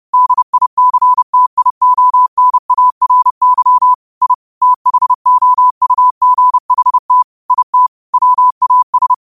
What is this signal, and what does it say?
Morse code, heard as a keyed tone, about 30 wpm, 1000 Hz. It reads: DICTIONARY I THOUGHT IT WAS